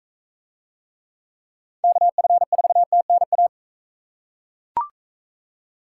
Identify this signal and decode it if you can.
Morse code, heard as a keyed tone, about 42 wpm, 700 Hz. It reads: KF4TNA